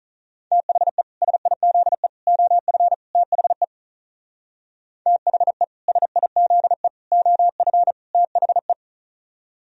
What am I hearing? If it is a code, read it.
Morse code, heard as a keyed tone, about 46 wpm, 700 Hz. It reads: THE SIZE OF THE THE SIZE OF THE